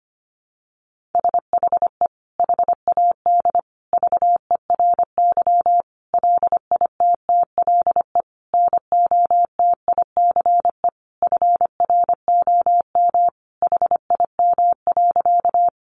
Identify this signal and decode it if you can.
Morse code, heard as a keyed tone, about 25 wpm, 700 Hz. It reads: SHE HAD VERY LITTLE NOTICE FROM HIM.
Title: Sentences from Top 400 Words 25wpm